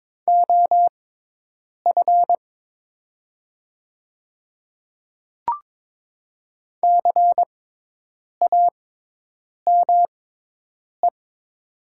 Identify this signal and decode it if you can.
Morse code, heard as a keyed tone, about 22 wpm, spaced as for 8 wpm, 700 Hz. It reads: OF CAME
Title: 6x Character Spacing - Top 200 Words - 22wpm